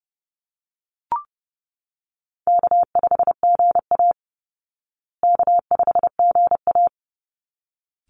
Morse code, heard as a keyed tone, about 30 wpm, 700 Hz. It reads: K5GA K5GA
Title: Easy Call Signs Repeated 2x - 30wpm